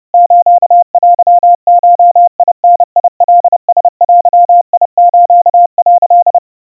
Morse code (Japanese Wabun code), heard as a keyed tone, about 30 wpm, 700 Hz. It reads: ステゴダカラデス」
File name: 110 cpm 700 Hz Crispy